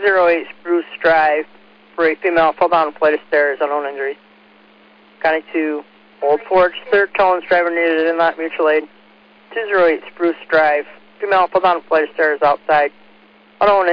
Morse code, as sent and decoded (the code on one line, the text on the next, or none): none